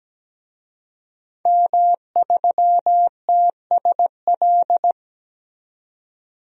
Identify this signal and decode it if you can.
Morse code, heard as a keyed tone, about 17 wpm, 700 Hz. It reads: M3TSL
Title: DXCC-100 Call Signs - 5 Characters - 17wpm